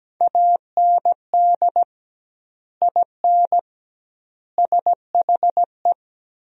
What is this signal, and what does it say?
Morse code, heard as a keyed tone, about 17 wpm, 700 Hz. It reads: AND IN SHE
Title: 2x Word-Spacing - Sentences from Top 200 Words - 17wpm